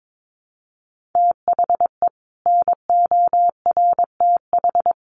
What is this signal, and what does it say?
Morse code, heard as a keyed tone, about 22 wpm, 700 Hz. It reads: THE NORTH
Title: Sets of 5 Words Encore 22wpm